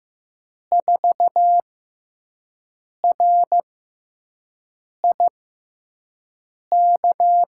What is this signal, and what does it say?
Morse code, heard as a keyed tone, about 15 wpm, spaced as for 5 wpm, 700 Hz. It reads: 4RIK
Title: USA Call Signs - 5 Characters - 6x Character Spacing - 15wpm